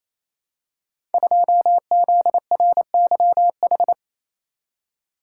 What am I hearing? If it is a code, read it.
Morse code, heard as a keyed tone, about 28 wpm, 700 Hz. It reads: 2ZRYH
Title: Code Groups - 5 Letter:Numbers - 28wpm